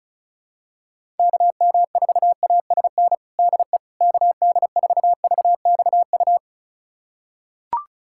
Morse code, heard as a keyed tone, about 34 wpm, 700 Hz. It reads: KM4ASN DE KD4VXU
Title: USA Callsign DE Callsign - 6 Characters - 35wpm